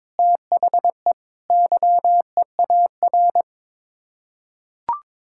Morse code, heard as a keyed tone, about 22 wpm, 700 Hz. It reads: THE YEAR
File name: practice / 22wpm / Sets of 6 Words - 22wpm